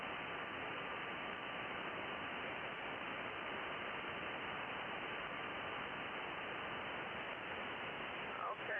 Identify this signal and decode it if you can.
FT8, with speech.